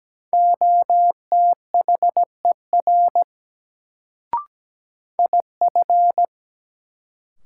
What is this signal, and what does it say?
Morse code, heard as a keyed tone, about 17 wpm, 700 Hz. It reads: OTHER IF
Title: Sentences from Top 100 Words - New Word Drill - 17wpm